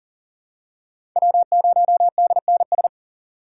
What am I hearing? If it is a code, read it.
Morse code, heard as a keyed tone, about 42 wpm, 700 Hz. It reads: W0DNS
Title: USA Call Signs - 5 Characters - 6x Character Spacing - 40wpm